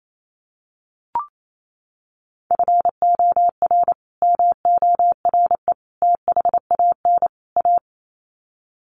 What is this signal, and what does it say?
Morse code, heard as a keyed tone, about 28 wpm, 700 Hz. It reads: FOR MORE THAN A
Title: Sets of 4 Words 28wpm